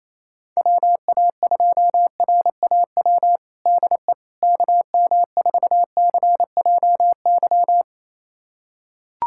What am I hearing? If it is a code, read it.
Morse code, heard as a keyed tone, about 27 wpm, 700 Hz. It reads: WA2RAW DE KM4CJY
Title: USA Callsign DE Callsign - 6 Characters - 28wpm